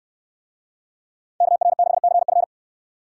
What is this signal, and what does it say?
Morse code, heard as a keyed tone, about 69 wpm, 700 Hz. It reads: KR6CF